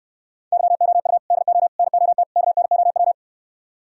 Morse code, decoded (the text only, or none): YOU DO NOT KNOW